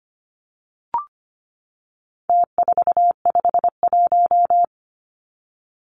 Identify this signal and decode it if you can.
Morse code, heard as a keyed tone, about 25 wpm, 700 Hz. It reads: T451
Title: Numbers - Contest Style - 4-Digit 25wpm